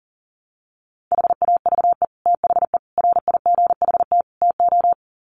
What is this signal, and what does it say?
Morse code, heard as a keyed tone, about 40 wpm, 700 Hz. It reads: HAVE THE RIGHT TO
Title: Sets of 4 Words 40wpm